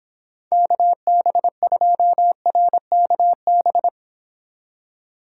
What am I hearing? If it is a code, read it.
Morse code, heard as a keyed tone, about 26 wpm, 700 Hz. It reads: KB2RKB